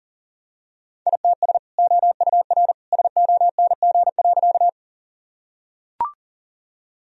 Morse code, heard as a keyed tone, about 40 wpm, 700 Hz. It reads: ITS OUR SONG.
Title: Sentences from Top 1000 Words - Review Part 2 - 40wpm